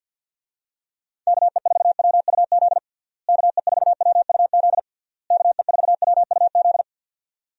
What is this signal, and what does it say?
Morse code, heard as a keyed tone, about 49 wpm, 700 Hz. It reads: KE4WUZ KE4WUZ KE4WUZ
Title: USA Call Signs - 6 Characters - Repeated 3x - 50wpm